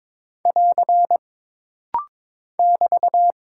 Morse code, heard as a keyed tone, about 22 wpm, 700 Hz. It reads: <AR> <BT>
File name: practice / 22wpm / Single Punctuation-Prosign - Rapid-Fire - 22wpm